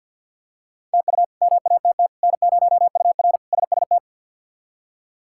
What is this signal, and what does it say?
Morse code, heard as a keyed tone, about 52 wpm, 700 Hz. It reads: TU MATT N0UR SST
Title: K1USN SST QSO Practice - 50wpm